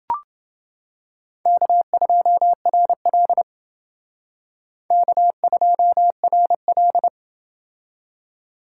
Morse code, heard as a keyed tone, about 29 wpm, 700 Hz. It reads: K2RL K2RL